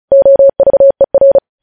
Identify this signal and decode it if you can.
Morse code, heard as a keyed tone, about 36 wpm, 550 Hz. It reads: OVER
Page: news headlines in morse code at 35 wpm